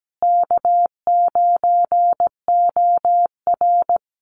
Morse code, heard as a keyed tone, about 17 wpm, 700 Hz. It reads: K9OR
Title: Call Signs Repeated 2x - 17wpm